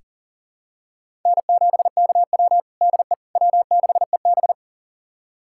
Morse code, heard as a keyed tone, about 40 wpm, 700 Hz. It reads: N7KW DE W6EB
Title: USA Callsign DE Callsign - 4 Characters - 40wpm